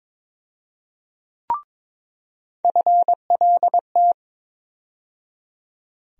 Morse code, heard as a keyed tone, about 22 wpm, 700 Hz. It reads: FLT
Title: USA Call Sign Postfixes - 3 Letters 22wpm